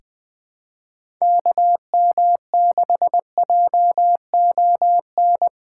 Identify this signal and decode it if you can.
Morse code, heard as a keyed tone, about 21 wpm, 700 Hz. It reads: KM6JON
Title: USA Call Signs - 6 Characters - 4x Character Spacing - 20wpm